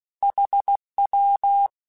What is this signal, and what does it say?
Morse code, heard as a keyed tone, about 16 wpm, 800 Hz. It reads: HW
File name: QSOs / QSOs-mp3s / WD0BGP 2@15